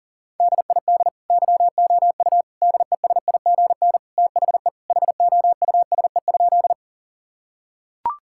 Morse code, heard as a keyed tone, about 40 wpm, 700 Hz. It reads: DID YOU DESIGN THE HOUSE?
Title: Sentences from Top 1000 Words - Review Part 5 - 40wpm